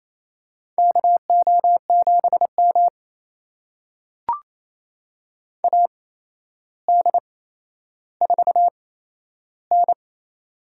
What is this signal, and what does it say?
Morse code, heard as a keyed tone, about 29 wpm, spaced as for 12 wpm, 700 Hz. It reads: KO7M AD4N
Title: USA Call Signs - 4 Characters - 8x Character Spacing - 28wpm